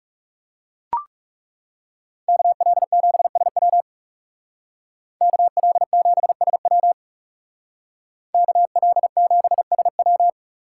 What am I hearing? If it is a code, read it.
Morse code, heard as a keyed tone, about 39 wpm, 700 Hz. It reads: KL7SW KL7SW KL7SW